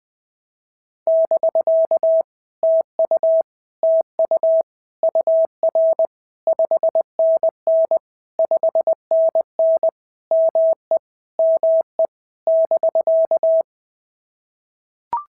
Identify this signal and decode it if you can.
Morse code, heard as a keyed tone, about 20 wpm, 650 Hz. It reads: <BK> TU TU UR 5NN 5NN ME ME <BK>
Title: POTA QSO Practice - 20wpm